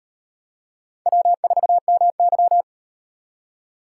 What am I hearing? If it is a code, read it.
Morse code, heard as a keyed tone, about 35 wpm, 700 Hz. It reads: W4MY